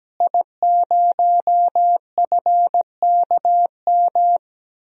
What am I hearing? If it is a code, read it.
Morse code, heard as a keyed tone, about 17 wpm, 700 Hz. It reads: I0FKM
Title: DXCC-100 Call Signs - 5 Characters - Repeated 3x - 17wpm